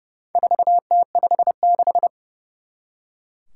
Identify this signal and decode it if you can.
Morse code, heard as a keyed tone, about 31 wpm, 700 Hz. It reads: 4T56
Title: Four Challenging Letter-Number Combinations 30wpm